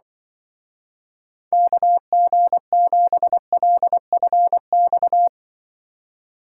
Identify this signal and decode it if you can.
Morse code, heard as a keyed tone, about 24 wpm, 700 Hz. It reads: KG7LFX
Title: Speed Racing - USA 2x3 Call Signs 20wpm